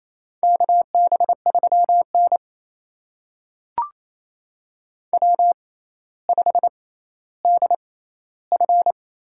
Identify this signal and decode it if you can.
Morse code, heard as a keyed tone, about 28 wpm, spaced as for 14 wpm, 700 Hz. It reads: KB3N W5DF
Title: USA Call Signs - 4 Characters - 6x Character Spacing - 28wpm